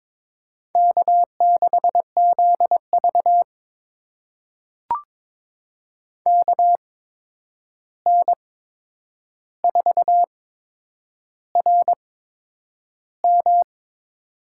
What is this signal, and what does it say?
Morse code, heard as a keyed tone, about 22 wpm, spaced as for 9 wpm, 700 Hz. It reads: K6ZV KN4RM
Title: USA Call Signs - 8x Character Spacing - 22wpm